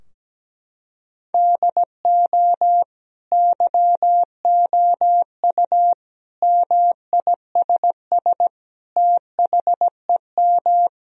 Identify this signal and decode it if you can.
Morse code, heard as a keyed tone, about 17 wpm, 700 Hz. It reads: DO YOU MISS THEM
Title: Sentences from Top 1000 Words - Review Part 3 - 17wpm